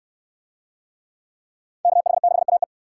Morse code, heard as a keyed tone, about 67 wpm, 700 Hz. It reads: KH6AE